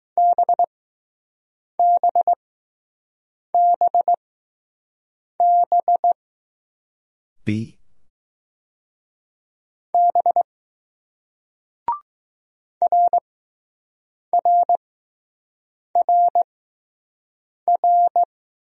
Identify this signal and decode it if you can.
Morse code, heard as a keyed tone, about 20 wpm, spaced as for 6 wpm, 700 Hz. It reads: BBBB B RRRR